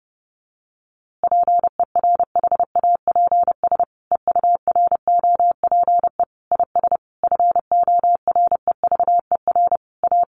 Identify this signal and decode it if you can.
Morse code, heard as a keyed tone, about 30 wpm, 700 Hz. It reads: PERHAPS EUROPE IS FOREVER A
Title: A Sound Of Thunder 30wpm